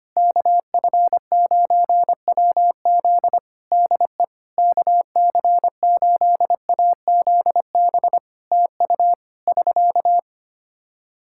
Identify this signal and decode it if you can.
Morse code, heard as a keyed tone, about 25 wpm, 700 Hz. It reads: KF9WZ DE KC8AZB TU <SK>
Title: Practice QSO - Exchange 4 - Ending - 25wpm